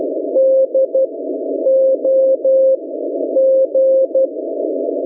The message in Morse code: DOG